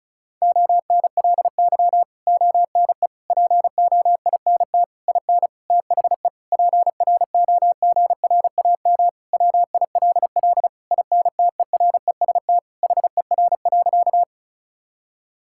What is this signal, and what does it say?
Morse code, heard as a keyed tone, about 34 wpm, 700 Hz. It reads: ONLY ONE POINT IN THE PROGRAM WILL INTEREST HER.
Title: Sentences from Top 1000 Words - Review Part 5 - 35wpm